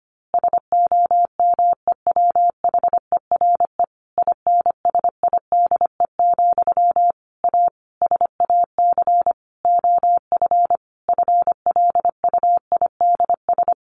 Morse code, heard as a keyed tone, about 25 wpm, 700 Hz. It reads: SOMEWHERE INSIDE, A SAC OF FLUIDS
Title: A Sound Of Thunder 25wpm